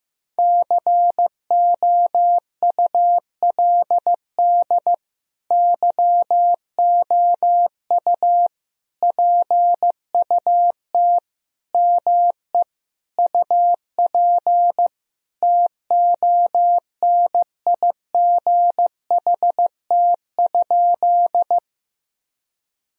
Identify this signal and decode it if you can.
Morse code, heard as a keyed tone, about 15 wpm, 700 Hz. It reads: COULD YOU PUT ME UP TONIGHT?
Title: Sentences from Top 1000 Words - Review Part 3 - 15wpm